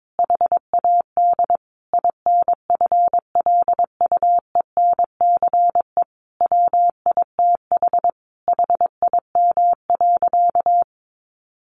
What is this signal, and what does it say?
Morse code, heard as a keyed tone, about 22 wpm, 700 Hz. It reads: HAD INFLUENCE WITH HIM.
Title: Sentences from Top 1100 Words 22wpm